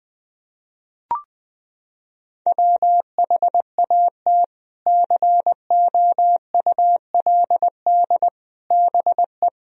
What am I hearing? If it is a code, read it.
Morse code, heard as a keyed tone, about 20 wpm, 700 Hz. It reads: WHAT COULD BE
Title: Sentences from Top 400 Words and Abbreviations 20wpm